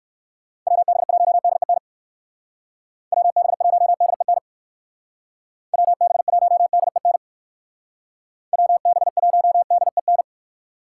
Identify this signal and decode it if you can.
Morse code, heard as a keyed tone, about 53 wpm, 700 Hz. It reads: WB1DEN WB1DEN WB1DEN WB1DEN
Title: Speed Racing - USA Call Signs 45wpm